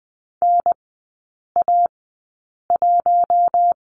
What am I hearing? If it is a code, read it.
Morse code, heard as a keyed tone, about 20 wpm, spaced as for 8 wpm, 700 Hz. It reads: NA1
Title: 114 Lesson 19e - Letter w - Three Characters 20wpm